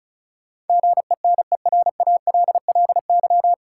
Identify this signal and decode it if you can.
Morse code, heard as a keyed tone, about 36 wpm, 700 Hz. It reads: GENERALLY
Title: Sentences from Top 600 Words - New Word Drill - 35wpm